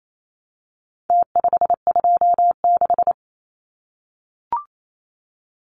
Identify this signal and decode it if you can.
Morse code, heard as a keyed tone, about 28 wpm, 700 Hz. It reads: T526
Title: Numbers - Contest Style - 4-Digit 28wpm